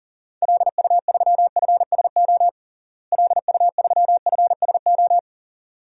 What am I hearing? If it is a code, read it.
Morse code, heard as a keyed tone, about 39 wpm, 700 Hz. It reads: LU3FSO LU3FSO
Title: DXCC-100 Call Signs - 6 Characters - Repeated 2x - 40wpm